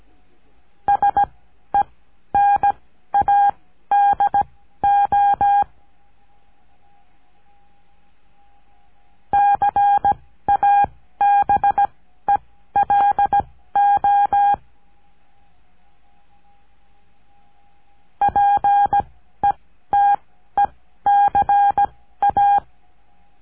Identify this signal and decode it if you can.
Morse code, heard as a keyed tone, about 17 wpm, spaced as for 12 wpm, 810 Hz. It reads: SENADO CABELO PETECA
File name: cursos / CURSO DE TELEGRAFIA / Telegrafia - Trilha 14